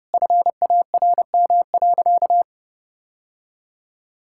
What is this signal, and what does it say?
Morse code, heard as a keyed tone, about 30 wpm, 700 Hz. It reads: FARM.
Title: Sentences from Top 1000 Words - Review Part 6 - 30wpm